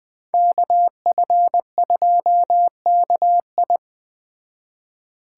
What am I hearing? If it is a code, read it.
Morse code, heard as a keyed tone, about 19 wpm, 700 Hz. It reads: KF2KI